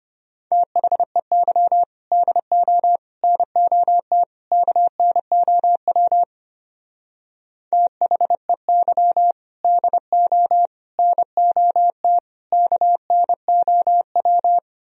Morse code, heard as a keyed tone, about 32 wpm, 700 Hz. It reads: THEY DO NOT KNOW THEY DO NOT KNOW